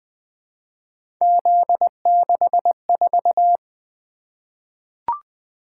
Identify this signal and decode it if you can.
Morse code, heard as a keyed tone, about 20 wpm, 700 Hz. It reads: Z64
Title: Three Letter-Number Combinations - 20wpm